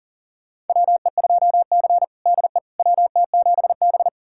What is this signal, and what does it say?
Morse code, heard as a keyed tone, about 40 wpm, 700 Hz. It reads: WE2C DE WT7B
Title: USA Callsign DE Callsign - 4 Characters - 40wpm